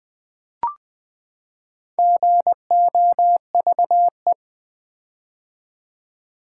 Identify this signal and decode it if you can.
Morse code, heard as a keyed tone, about 21 wpm, 700 Hz. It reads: GOVE
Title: Build A Word - 4 Letters At A Time - 10-Letter Words - 20wpm